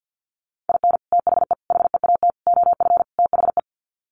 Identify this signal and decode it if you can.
Morse code, heard as a keyed tone, about 50 wpm, 700 Hz. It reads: IN THE HEAT OF THE
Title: Sets of 5 Words Encore2 50wpm